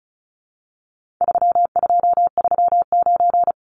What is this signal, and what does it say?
Morse code, heard as a keyed tone, about 34 wpm, 700 Hz. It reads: 3239
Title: Numbers - 4-Digit 35wpm